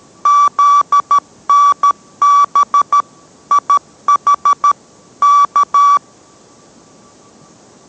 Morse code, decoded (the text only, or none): ZNB IH K